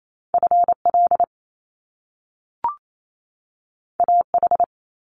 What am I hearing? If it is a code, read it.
Morse code, heard as a keyed tone, about 28 wpm, 700 Hz. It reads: FL AH
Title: Top Two Letter Combinations in Words 28wpm